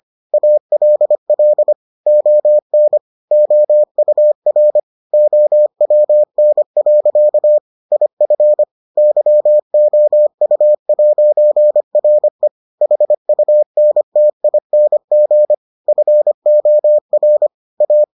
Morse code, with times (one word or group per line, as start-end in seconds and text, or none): 0.3-1.7 ALL
2.1-3.0 ON
3.3-4.8 OUR
5.1-7.6 OWN.
7.9-8.6 IF
9.0-12.5 YOU'RE
12.8-15.5 HUNTING
15.9-17.5 FOR
17.8-18.0 A